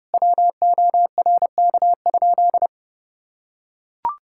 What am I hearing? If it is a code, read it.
Morse code, heard as a keyed tone, about 30 wpm, 700 Hz. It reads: WORK?
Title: Sentences from Top 1000 Words - Review Part 6 - 30wpm